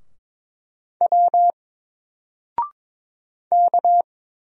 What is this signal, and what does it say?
Morse code, heard as a keyed tone, about 22 wpm, 700 Hz. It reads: W K
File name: Single-Letter-Number-22wpm